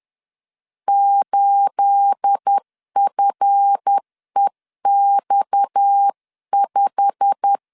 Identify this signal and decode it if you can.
Morse code, heard as a keyed tone, about 10 wpm, 800 Hz. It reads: 8FEX5
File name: LEZIONI / 9 28 10wpm 2842